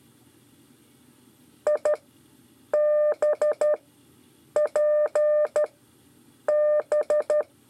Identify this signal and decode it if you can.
Morse code, heard as a keyed tone, about 12 wpm, spaced as for 7 wpm, 600 Hz. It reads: IBP B
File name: 5 LEZIONE RIPASSO 1